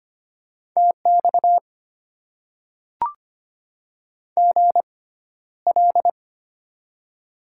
Morse code, heard as a keyed tone, about 25 wpm, spaced as for 13 wpm, 700 Hz. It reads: TX GL
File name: Top 90 QSO Elements - 6x Character Spacing - 25wpm